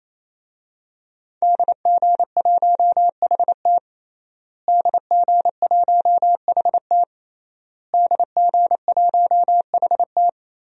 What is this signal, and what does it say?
Morse code, heard as a keyed tone, about 28 wpm, 700 Hz. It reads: DG1HT DG1HT DG1HT
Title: DXCC-100 Call Signs - 5 Characters - Repeated 3x - 28wpm